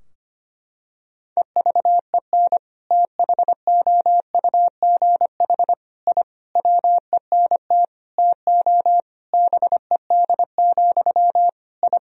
Morse code, heard as a keyed tone, about 25 wpm, 700 Hz. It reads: EVEN THOUGH I WENT TO BED, I
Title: Sentences from Top 1000 Words - Review Part 2 - 25wpm